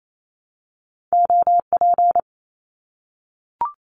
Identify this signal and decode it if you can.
Morse code, heard as a keyed tone, about 28 wpm, 700 Hz. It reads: OP
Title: Abbreviations 28wpm